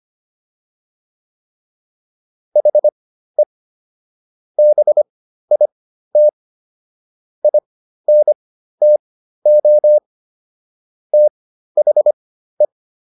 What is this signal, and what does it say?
Morse code, heard as a keyed tone, about 25 wpm, spaced as for 13 wpm, 600 Hz. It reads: HE BIT INTO THE